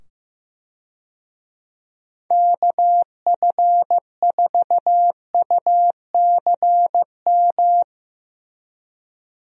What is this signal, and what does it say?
Morse code, heard as a keyed tone, about 16 wpm, 700 Hz. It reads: KF4UCM
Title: USA Call Signs - 6 Characters - 15wpm